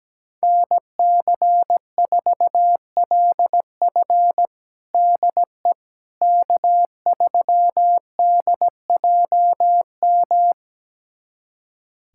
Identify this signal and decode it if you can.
Morse code, heard as a keyed tone, about 17 wpm, 700 Hz. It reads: NC4LF DE K3DJM
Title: USA Callsign DE Callsign - 5 Characters - 17wpm